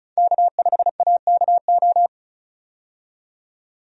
Morse code, heard as a keyed tone, about 36 wpm, 700 Hz. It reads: K5AKO
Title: USA Call Signs - 5 Characters - 35wpm